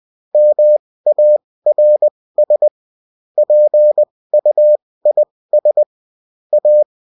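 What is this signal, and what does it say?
Morse code, heard as a keyed tone, about 20 wpm, spaced as for 16 wpm, 600 Hz. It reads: MARS PUIS A
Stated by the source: Morse code bulletin of a French amateur CW union, sent as text and in CW.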